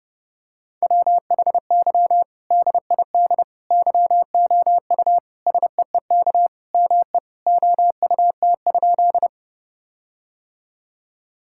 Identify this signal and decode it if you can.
Morse code, heard as a keyed tone, about 30 wpm, 700 Hz. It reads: WHY DID YOU SEEK ME OUT?
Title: Sentences from Top 1600 Words 30wpm